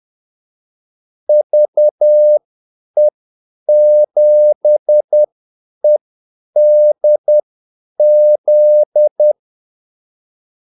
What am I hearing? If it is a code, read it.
Morse code, heard as a keyed tone, about 10 wpm, spaced as for 8 wpm, 600 Hz. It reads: VE7EDZ
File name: Callsigns-10-8wpm